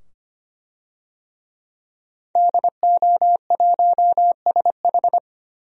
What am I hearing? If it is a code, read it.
Morse code, heard as a keyed tone, about 25 wpm, 700 Hz. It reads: DO1SH
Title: DXCC-100 Call Signs - 5 Characters - 25wpm